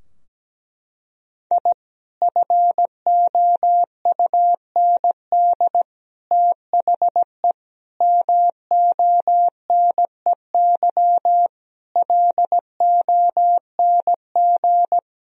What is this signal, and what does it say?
Morse code, heard as a keyed tone, about 17 wpm, 700 Hz. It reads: I FOUND THE MONEY LONG
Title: Sentences from Top 1000 Words - Review Part 3 - 17wpm